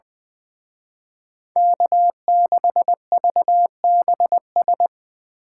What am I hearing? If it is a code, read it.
Morse code, heard as a keyed tone, about 25 wpm, 700 Hz. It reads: K6VBS
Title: Speed Racing - USA 1x3 Call Signs 20wpm